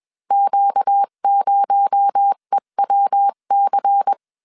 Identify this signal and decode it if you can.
Morse code, heard as a keyed tone, about 21 wpm, 800 Hz. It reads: Q0EWC